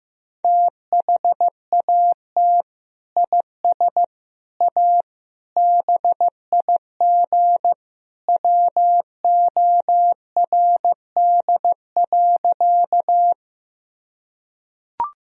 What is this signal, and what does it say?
Morse code, heard as a keyed tone, about 15 wpm, 700 Hz. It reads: THAT IS A BIG WORD.
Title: Sentences from Top 1000 Words - Review Part 1 - 15wpm